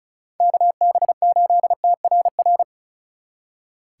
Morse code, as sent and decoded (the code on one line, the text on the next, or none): -.- -... ---.. - .-. .-.
KB8TRR